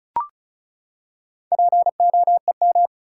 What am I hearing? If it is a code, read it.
Morse code, heard as a keyed tone, about 36 wpm, 700 Hz. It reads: POEM
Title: Four Letter Words - 35wpm